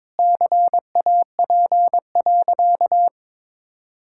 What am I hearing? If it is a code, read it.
Morse code, heard as a keyed tone, about 22 wpm, 700 Hz. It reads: CAP.
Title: Sentences from Top 1600 Words 22wpm